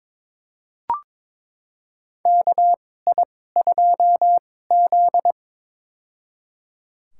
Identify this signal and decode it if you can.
Morse code, heard as a keyed tone, about 22 wpm, spaced as for 16 wpm, 700 Hz. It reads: KI2Z
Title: USA Call Signs - 4 Characters - 2x Character Spacing - 22wpm